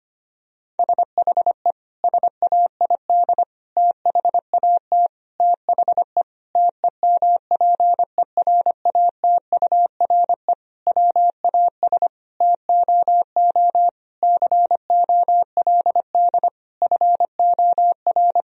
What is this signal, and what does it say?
Morse code, heard as a keyed tone, about 25 wpm, 700 Hz. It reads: SHE SAID THAT THE TEMPERATURE WAS TOO COLD FOR
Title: Sentences from Top 1000 Words - Review Part 4 - 25wpm